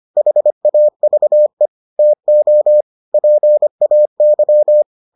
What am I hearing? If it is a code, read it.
Morse code, heard as a keyed tone, about 25 wpm, 600 Hz. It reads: HAVE TO PAY